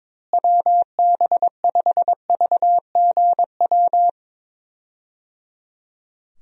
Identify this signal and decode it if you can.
Morse code, heard as a keyed tone, about 22 wpm, 700 Hz. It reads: WB5VGW
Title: USA Call Signs - 6 Characters - Repeated 3x - 22wpm